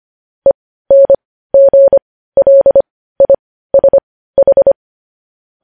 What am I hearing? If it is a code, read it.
Morse code, heard as a keyed tone, about 25 wpm, spaced as for 15 wpm, 550 Hz. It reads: ENGLISH
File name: podcast / News headlines-1x-25x15